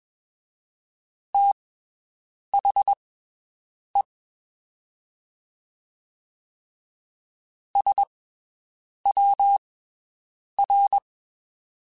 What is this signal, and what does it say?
Morse code, heard as a keyed tone, about 21 wpm, spaced as for 7 wpm, 800 Hz. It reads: THE SWR
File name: (spaced) / QSOs Spaced-mp3s / WD0FBT 1 @20 sp